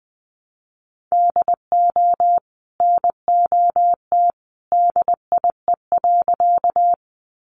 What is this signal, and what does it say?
Morse code, heard as a keyed tone, about 20 wpm, 700 Hz. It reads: DO NOT DIE.
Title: Sentences from Top 1300 Words 20wpm